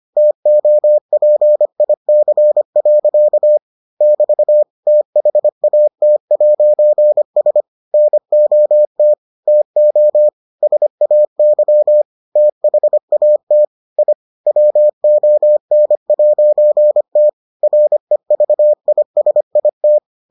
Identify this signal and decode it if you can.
Morse code, heard as a keyed tone, about 25 wpm, 600 Hz. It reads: TOPIC. <BT> THAT'S NOT TO SAY THAT I WON'T REVISIT